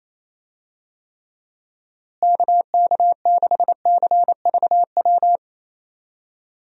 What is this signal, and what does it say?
Morse code, heard as a keyed tone, about 28 wpm, 700 Hz. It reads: KK6CVW
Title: USA Call Signs - 6 Characters - Repeated 3x - 28wpm